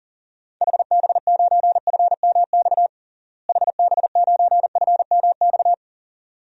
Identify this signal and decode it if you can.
Morse code, heard as a keyed tone, about 40 wpm, 700 Hz. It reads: HB9FMX HB9FMX
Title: DXCC-100 Call Signs - 6 Characters - Repeated 2x - 40wpm